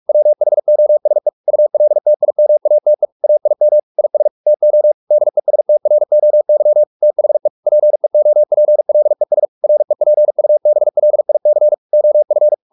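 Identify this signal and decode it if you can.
Morse code, heard as a keyed tone, about 44 wpm, 600 Hz. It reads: WHOSE ULTIMATE AIM IS TO DESTROY THE PEOPLES REPUBLIC OF
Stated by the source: Morse practice at 45 wpm, the novel Never by Ken Follett in English, from CW med Gnister.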